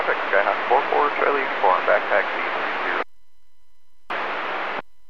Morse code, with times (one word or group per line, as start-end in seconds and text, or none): none